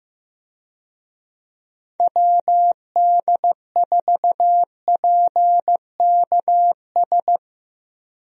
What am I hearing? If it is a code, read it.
Morse code, heard as a keyed tone, about 15 wpm, 700 Hz. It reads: WD4PKS
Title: USA Call Signs - 6 Characters - Repeated 3x - 15wpm